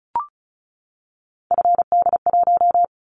Morse code, heard as a keyed tone, about 36 wpm, 700 Hz. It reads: FD1
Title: Callsign Prefixes for Top 25 DXCC - 35wpm